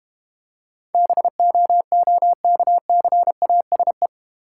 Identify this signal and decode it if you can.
Morse code, heard as a keyed tone, about 31 wpm, 700 Hz. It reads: BOOKCASE
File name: Easy Words - Speed-Racing with Hints - 25wpm